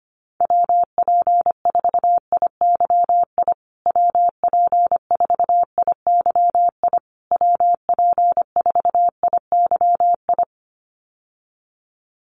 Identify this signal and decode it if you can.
Morse code, heard as a keyed tone, about 25 wpm, 700 Hz. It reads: WP4IYI WP4IYI WP4IYI
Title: USA Call Signs Repeated 3x 25wpm